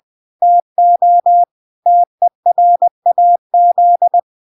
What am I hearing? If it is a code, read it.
Morse code, heard as a keyed tone, about 20 wpm, 700 Hz. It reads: TO TERAZ